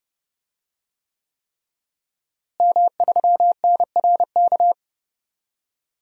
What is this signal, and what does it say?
Morse code, heard as a keyed tone, about 30 wpm, 700 Hz. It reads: M3NRK
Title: DXCC-100 Call Signs - 5 Characters - Repeated 3x - 30wpm